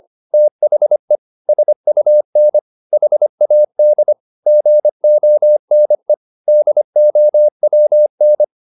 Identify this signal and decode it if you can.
Morse code, heard as a keyed tone, about 25 wpm, 600 Hz. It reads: THE SUN HAD GONE DOWN